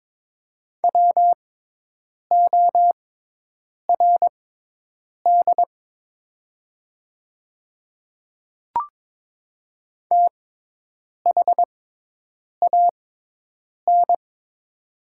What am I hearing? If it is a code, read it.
Morse code, heard as a keyed tone, about 22 wpm, spaced as for 8 wpm, 690 Hz. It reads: WORD THAN
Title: 6x Character Spacing - Top 100 Words - 22wpm